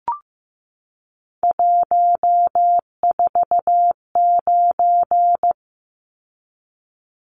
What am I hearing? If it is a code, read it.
Morse code, heard as a keyed tone, about 15 wpm, 700 Hz. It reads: 149
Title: Numbers - 3-Digit 15wpm